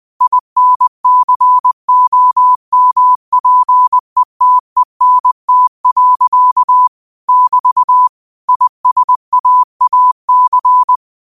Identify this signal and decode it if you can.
Morse code, heard as a keyed tone, about 20 wpm, 1000 Hz. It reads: INCOMPETENT. <BT> ISAAC
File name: qotdfiles / 20wpm